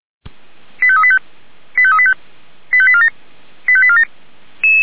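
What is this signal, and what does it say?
Five-tone selective call (ZVEI), heard as quick runs of five tones, tones of about 70 ms. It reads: 85385 85385 85847 85847
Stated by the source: ZVEI five-tone selective call received by SDR.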